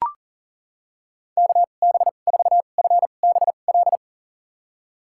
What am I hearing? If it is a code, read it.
Morse code, heard as a keyed tone, about 40 wpm, spaced as for 29 wpm, 700 Hz. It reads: KB4FBL
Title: USA Call Signs - 6 Characters - 2x Character Spacing - 40wpm